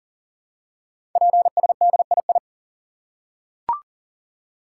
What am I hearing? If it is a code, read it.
Morse code, heard as a keyed tone, about 38 wpm, 700 Hz. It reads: PSDII
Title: Code Groups - 5 Letter:Numbers - 40wpm